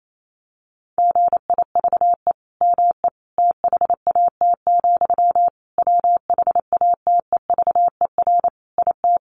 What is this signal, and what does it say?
Morse code, heard as a keyed tone, about 28 wpm, 700 Hz. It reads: GIVE ME THAT, WHATEVER IT
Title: Sentences from Top 1100 Words 28wpm